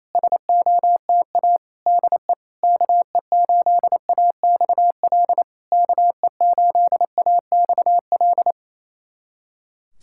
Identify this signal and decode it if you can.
Morse code, heard as a keyed tone, about 27 wpm, 690 Hz. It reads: SOTA DE KE8AXL KE8AXL
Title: SOTA QSO Practice 28wpm